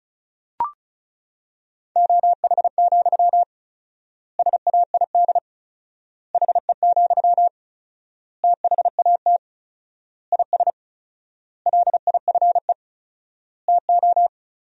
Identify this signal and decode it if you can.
Morse code, heard as a keyed tone, about 34 wpm, 700 Hz. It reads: OH, SAID HE, THAT IS LIFE TO
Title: 4x Word-Spacing - Sentences from Top 500 Words - 35wpm